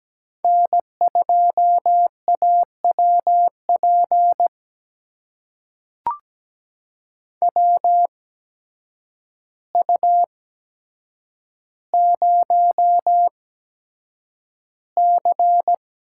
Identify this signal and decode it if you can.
Morse code, heard as a keyed tone, about 17 wpm, spaced as for 8 wpm, 700 Hz. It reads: N2AWP WU0C
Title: USA Call Signs - 8x Character Spacing - 17wpm